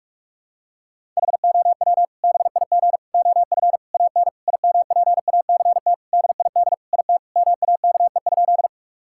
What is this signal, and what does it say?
Morse code, heard as a keyed tone, about 41 wpm, 700 Hz. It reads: HOW BIG OF AN IMPACT DID IT MAKE?
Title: Sentences from Top 1000 Words - Review Part 6 - 45wpm